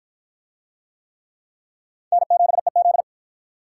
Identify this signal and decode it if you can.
Morse code, heard as a keyed tone, about 52 wpm, 700 Hz. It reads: N7EZ